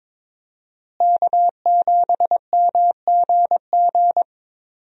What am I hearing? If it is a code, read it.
Morse code, heard as a keyed tone, about 22 wpm, 700 Hz. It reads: K7MGG
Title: USA Call Signs - 5 Characters - Repeated 3x - 22wpm